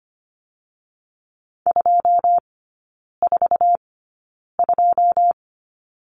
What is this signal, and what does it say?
Morse code, heard as a keyed tone, about 25 wpm, spaced as for 9 wpm, 690 Hz. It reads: 242
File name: Lesson 14e - Number 2 - Three Characters 25wpm